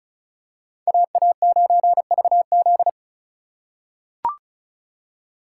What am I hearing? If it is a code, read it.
Morse code, heard as a keyed tone, about 34 wpm, 700 Hz. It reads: AA9VZ
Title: USA Call Signs - 5 Characters - 2x Character Spacing - 35wpm